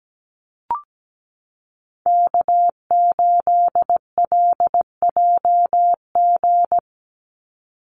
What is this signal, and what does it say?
Morse code, heard as a keyed tone, about 17 wpm, 700 Hz. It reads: K8LJG
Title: NA SOTA Chaser Callsigns 17wpm